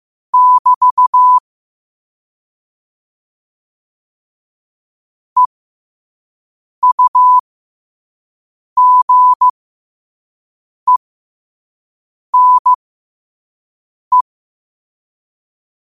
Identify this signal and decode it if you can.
Morse code, heard as a keyed tone, about 15 wpm, spaced as for 5 wpm, 1000 Hz. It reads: <BT> EUGENE
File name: qotdfiles / 5wpm